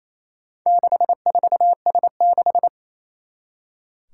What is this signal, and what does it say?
Morse code, heard as a keyed tone, about 27 wpm, 700 Hz. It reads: 64S6
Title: Four Challenging Letter-Number Combinations 28wpm